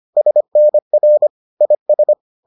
Morse code, heard as a keyed tone, about 25 wpm, 600 Hz. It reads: SNR IS